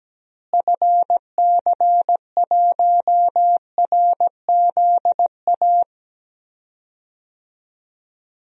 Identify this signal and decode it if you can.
Morse code, heard as a keyed tone, about 17 wpm, 700 Hz. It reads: FC1RZA
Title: DXCC-100 Call Signs - 6 Characters - Repeated 2x - 17wpm